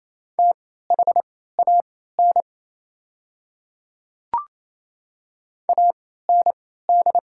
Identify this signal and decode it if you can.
Morse code, heard as a keyed tone, about 29 wpm, spaced as for 16 wpm, 690 Hz. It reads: THAN AND
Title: 3x Character Spacing - Top 100 Words - 28wpm